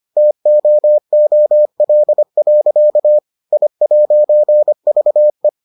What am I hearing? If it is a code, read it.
Morse code, heard as a keyed tone, about 25 wpm, 600 Hz. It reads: TOOL. I'VE